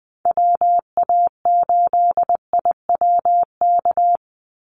Morse code, heard as a keyed tone, about 20 wpm, 700 Hz. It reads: WA8IWK
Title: Call Signs Repeated 2x - 20wpm